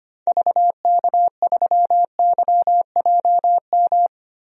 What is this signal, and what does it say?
Morse code, heard as a keyed tone, about 25 wpm, 700 Hz. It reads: VK3YJM
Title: DXCC-100 Call Signs - 6 Characters - Repeated 2x - 25wpm